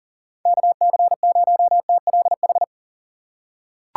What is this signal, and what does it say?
Morse code, heard as a keyed tone, about 40 wpm, 700 Hz. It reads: KC0TLH
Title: USA Call Signs - 3x Character Spacing - 40wpm